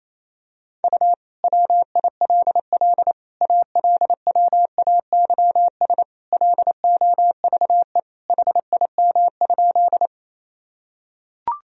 Morse code, heard as a keyed tone, about 28 wpm, 700 Hz. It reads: U WILL ALWAYS LOVE HIM?
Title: Sentences from Top 400 Words and Abbreviations 28wpm